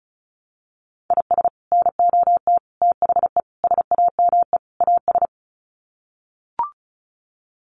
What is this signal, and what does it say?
Morse code, heard as a keyed tone, about 34 wpm, 700 Hz. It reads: IS NOT THE SAME AS
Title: Sets of 5 Words Encore 35wpm